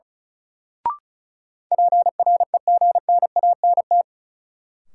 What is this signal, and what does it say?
Morse code, heard as a keyed tone, about 36 wpm, 700 Hz. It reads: PREGNANT
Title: Sentences from Top 1000 Words - Review Part 6 - New Word Drill - 35wpm